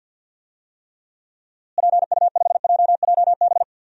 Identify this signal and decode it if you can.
Morse code, heard as a keyed tone, about 52 wpm, 700 Hz. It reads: PU5JJB
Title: DXCC-100 Call Signs - 6 Characters - Repeated 3x - 50wpm